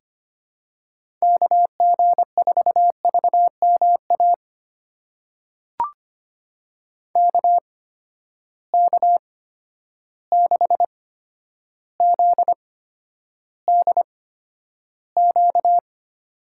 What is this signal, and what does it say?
Morse code, heard as a keyed tone, about 25 wpm, spaced as for 11 wpm, 700 Hz. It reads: KG4VMA KK6ZDQ
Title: USA Call Signs - 6 Characters - 8x Character Spacing - 25wpm